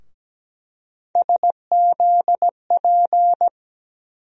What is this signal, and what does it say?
Morse code, heard as a keyed tone, about 17 wpm, 700 Hz. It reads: SZP
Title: USA Call Sign Postfixes - 3 Letters 17wpm